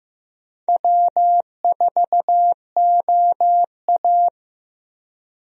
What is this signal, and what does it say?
Morse code, heard as a keyed tone, about 15 wpm, 700 Hz. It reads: W4OA
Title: USA Call Signs - 4 Characters - 15wpm